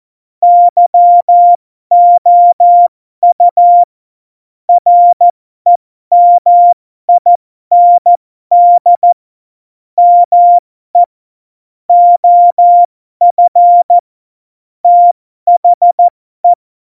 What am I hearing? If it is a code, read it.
Morse code, heard as a keyed tone, about 14 wpm, spaced as for 12 wpm, 700 Hz. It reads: YOU REMIND ME OF THE
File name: Morsekurs / Witze